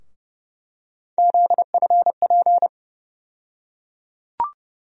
Morse code, heard as a keyed tone, about 31 wpm, 700 Hz. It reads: ZFP
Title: USA Call Sign Postfixes - 3 Letters 30wpm